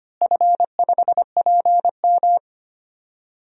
Morse code, heard as a keyed tone, about 25 wpm, 700 Hz. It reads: F5PM
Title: DXCC-100 Call Signs - 4 Characters - Repeated 3x - 25wpm